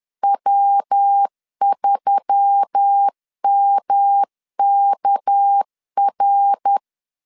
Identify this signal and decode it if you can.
Morse code, heard as a keyed tone, about 10 wpm, 800 Hz. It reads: W3MKR